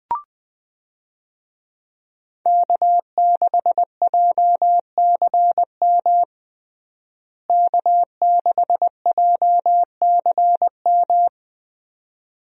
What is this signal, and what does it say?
Morse code, heard as a keyed tone, about 20 wpm, 700 Hz. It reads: K6JCM K6JCM
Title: USA Call Signs - 5 Characters - Repeated 2x - 20wpm